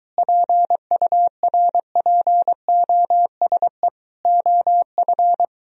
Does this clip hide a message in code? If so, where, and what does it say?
Yes: Morse code from 0.2 to 5.5 s, heard as a keyed tone, about 23 wpm, 700 Hz. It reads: PURPOSE OF